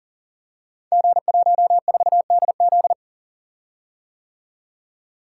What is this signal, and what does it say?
Morse code, heard as a keyed tone, about 39 wpm, 700 Hz. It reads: G14DZ
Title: Code Groups - 5 Letter:Numbers - 40wpm